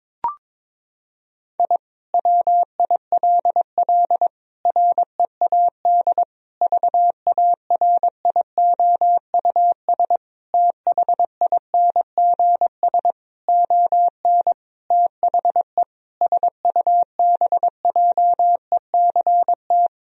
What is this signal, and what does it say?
Morse code, heard as a keyed tone, about 21 wpm, 700 Hz. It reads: I WILL READ VARIOUS THINGS ON THE SUBJECT
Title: Sentences from Top 1000 Words - Review Part 4 - 22wpm